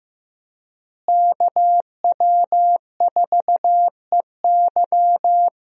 Morse code, heard as a keyed tone, about 15 wpm, 700 Hz. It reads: KW4EY
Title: USA Call Signs - 5 Characters - Repeated 3x - 15wpm